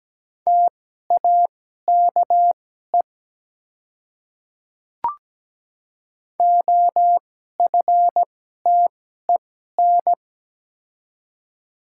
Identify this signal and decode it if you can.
Morse code, heard as a keyed tone, about 17 wpm, spaced as for 12 wpm, 700 Hz. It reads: TAKE OFTEN
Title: 2x Character Spacing - Top 300 Words - 17wpm